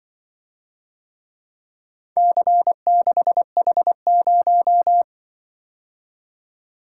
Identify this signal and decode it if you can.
Morse code, heard as a keyed tone, about 23 wpm, 700 Hz. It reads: C6H0
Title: Speed-Racing - Four Letter-Number Combinations - 20wpm